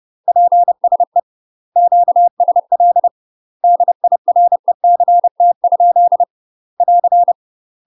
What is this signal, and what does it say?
Morse code, heard as a keyed tone, about 30 wpm, 700 Hz. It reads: PSE QSL DIRECT? <AR>